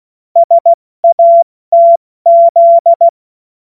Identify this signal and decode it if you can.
Morse code, heard as a keyed tone, about 16 wpm, 680 Hz. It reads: SATZ